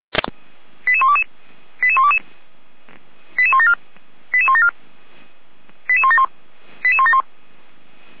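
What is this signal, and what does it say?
Five-tone selective call (ZVEI), heard as quick runs of five tones, tones of about 70 ms. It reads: 88122 88122 88174 88174 88171 88171
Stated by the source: ZVEI five-tone selective call received on an SDR.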